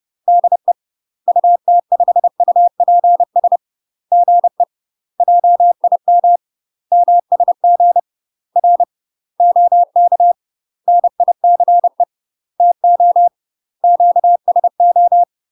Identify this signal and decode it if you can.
Morse code, heard as a keyed tone, about 30 wpm, 700 Hz. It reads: DE UT5UPS GE JIM MSG R OK NICE TO QSO
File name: UK MASTER QSOs TRAINING / 30WPM / UK MASTER QSOs 0001